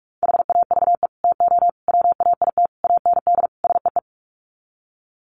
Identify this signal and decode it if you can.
Morse code, heard as a keyed tone, about 44 wpm, 700 Hz. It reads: HAVE TO WAIT AND SEE
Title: Sets of 5 Words Encore2 45wpm